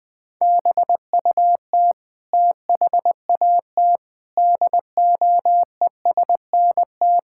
Morse code, heard as a keyed tone, about 21 wpm, 700 Hz. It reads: BUT THAT DOESNT
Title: Sentences with 7-Word Phrases - Spoken Beginning - 20wpm